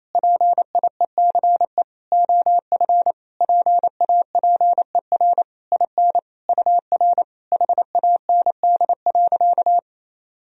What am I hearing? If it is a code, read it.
Morse code, heard as a keyed tone, about 27 wpm, 700 Hz. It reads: PIECE OF PAPER IN UR HAND.
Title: Sentences from Top 400 Words and Abbreviations 28wpm